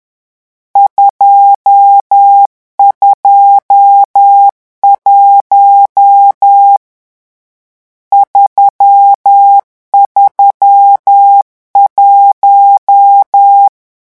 Morse code, heard as a keyed tone, about 10 wpm, 800 Hz. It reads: 221 331
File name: morse mp3 / lecon10-vitesse 10